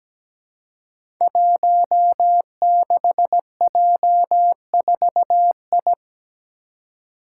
Morse code, heard as a keyed tone, about 17 wpm, 700 Hz. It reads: 16J4I
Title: Five Challenging Letter-Number Combinations 17wpm